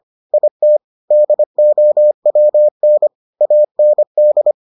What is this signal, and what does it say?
Morse code, heard as a keyed tone, about 25 wpm, 600 Hz. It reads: IT DOWN AND